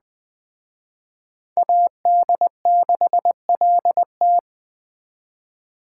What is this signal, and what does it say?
Morse code, heard as a keyed tone, about 21 wpm, 700 Hz. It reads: AD6LT
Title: Speed Racing - USA 2x2 Call Signs 17wpm